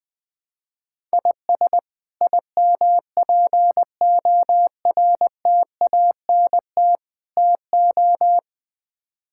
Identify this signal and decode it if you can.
Morse code, heard as a keyed tone, about 20 wpm, 700 Hz. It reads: IS IMPORTANT TO
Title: Build A Phrase - 5-Word Phrases - 20wpm